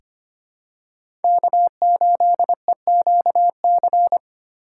Morse code, heard as a keyed tone, about 25 wpm, 700 Hz. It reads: K8EQC
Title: USA Call Signs - 5 Characters - 4x Character Spacing - 25wpm